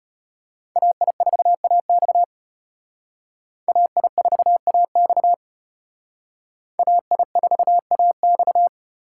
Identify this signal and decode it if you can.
Morse code, heard as a keyed tone, about 32 wpm, 700 Hz. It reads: AI4AX AI4AX AI4AX